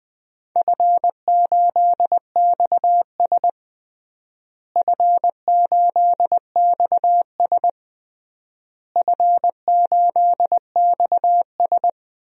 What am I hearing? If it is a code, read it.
Morse code, heard as a keyed tone, about 21 wpm, 700 Hz. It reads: F8XS F8XS F8XS
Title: DXCC-100 Call Signs - 4 Characters - Repeated 3x - 20wpm